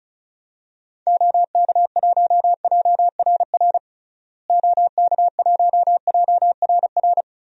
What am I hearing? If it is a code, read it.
Morse code, heard as a keyed tone, about 36 wpm, 700 Hz. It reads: OK1JRR OK1JRR
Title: DXCC-100 Call Signs - 6 Characters - Repeated 3x - 35wpm